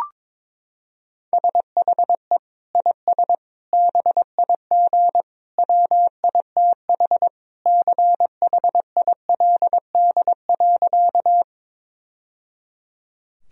Morse code, heard as a keyed tone, about 22 wpm, 700 Hz. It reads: SHE IS BIG WITH CHILD.
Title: Sentences from Top 1000 Words - Review Part 1 - 22wpm